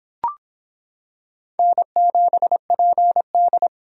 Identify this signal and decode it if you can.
Morse code, heard as a keyed tone, about 25 wpm, 700 Hz. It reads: N7PD